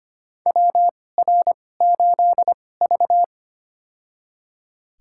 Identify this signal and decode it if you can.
Morse code, heard as a keyed tone, about 25 wpm, spaced as for 18 wpm, 700 Hz. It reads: WR8V
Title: USA Call Signs - 4 Characters - 2x Character Spacing - 25wpm